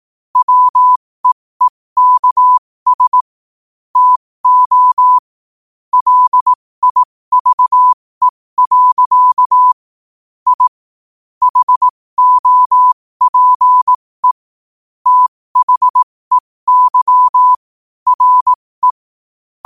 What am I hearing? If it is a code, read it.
Morse code, heard as a keyed tone, about 18 wpm, spaced as for 15 wpm, 1000 Hz. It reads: WEEKS TO LIVE. I HOPE THEY RE